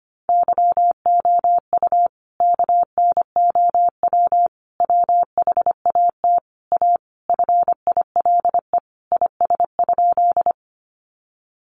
Morse code, heard as a keyed tone, about 25 wpm, 700 Hz. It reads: YOU KNOW WHAT A FILE IS?
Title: Sentences from Top 1200 Words 25wpm